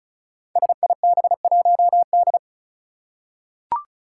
Morse code, heard as a keyed tone, about 36 wpm, 700 Hz. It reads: SIB1D
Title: Five Challenging Letter-Number Combinations 35wpm